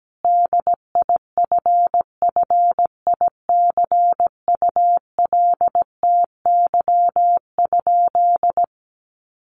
Morse code, heard as a keyed tone, about 17 wpm, 700 Hz. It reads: DIFFICULTY?
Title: Sentences from Top 1500 Words 17wpm